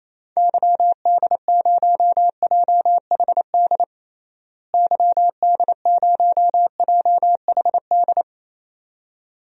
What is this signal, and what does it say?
Morse code, heard as a keyed tone, about 28 wpm, 700 Hz. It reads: YD0JHD YD0JHD
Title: DXCC-100 Call Signs - 6 Characters - Repeated 2x - 28wpm